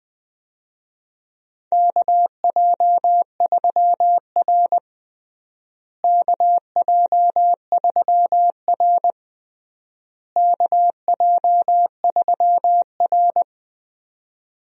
Morse code, heard as a keyed tone, about 20 wpm, 700 Hz. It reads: KJ3R KJ3R KJ3R
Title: USA Call Signs - 4 Characters - Repeated 3x - 20wpm